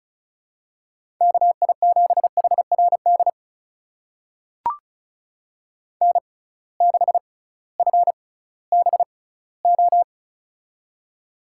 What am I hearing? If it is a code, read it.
Morse code, heard as a keyed tone, about 36 wpm, spaced as for 19 wpm, 700 Hz. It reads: KI7HRD N6FBO
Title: USA Call Signs - 6x Character Spacing - 35wpm